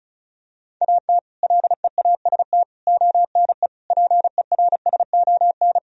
Morse code, heard as a keyed tone, about 36 wpm, 700 Hz. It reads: AT LEAST ONE PERSON
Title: Sentences from Top 1000 Words - Review Part 1 - 35wpm